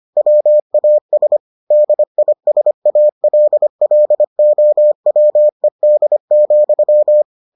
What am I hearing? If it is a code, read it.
Morse code, heard as a keyed tone, about 25 wpm, 600 Hz. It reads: WAS DISALLOWED,